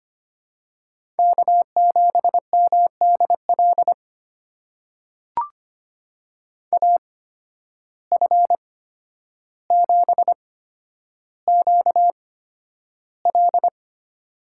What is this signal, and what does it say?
Morse code, heard as a keyed tone, about 25 wpm, spaced as for 11 wpm, 700 Hz. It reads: K7MDL AF7QL
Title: USA Call Signs - 5 Characters - 8x Character Spacing - 25wpm